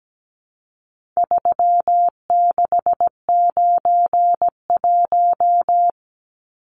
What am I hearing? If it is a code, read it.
Morse code, heard as a keyed tone, about 17 wpm, 700 Hz. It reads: 3691
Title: Numbers - 4-Digit 17wpm